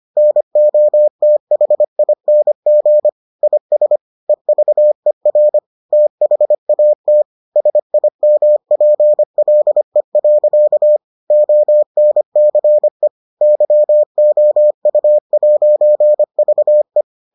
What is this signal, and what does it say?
Morse code, heard as a keyed tone, about 25 wpm, 600 Hz. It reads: NOTHING IS EVER THAT SIMPLE. ONCE YOU'VE